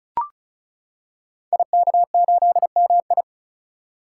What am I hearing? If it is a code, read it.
Morse code, heard as a keyed tone, about 36 wpm, 700 Hz. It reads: IK8MI
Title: Code Groups - 5 Letter:Numbers - 35wpm